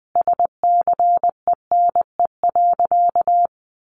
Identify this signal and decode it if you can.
Morse code, heard as a keyed tone, about 19 wpm, 700 Hz. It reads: SCENE.
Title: Sentences from Top 1300 Words 20wpm